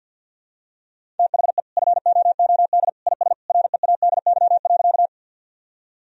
Morse code, heard as a keyed tone, about 49 wpm, 700 Hz. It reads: THE FOOD IS READY.
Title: Sentences from Top 1000 Words - Review Part 1 - 50wpm